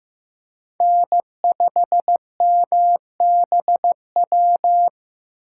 Morse code, heard as a keyed tone, about 15 wpm, 700 Hz. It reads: N5MBW
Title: USA Call Signs - 5 Characters - 4x Character Spacing - 15wpm